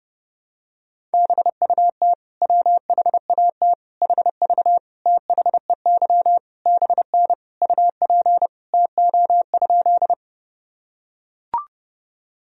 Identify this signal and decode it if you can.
Morse code, heard as a keyed tone, about 30 wpm, 700 Hz. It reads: BUT WHAT HV THEY BN UP TO?
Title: Sentences from Top 100 Words and Abbreviations 30wpm